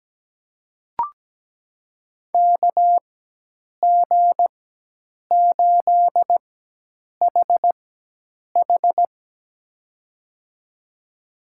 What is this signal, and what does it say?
Morse code, heard as a keyed tone, about 17 wpm, spaced as for 8 wpm, 700 Hz. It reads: KG8HH
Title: USA Call Signs - 5 Characters - 4x Character Spacing - 17wpm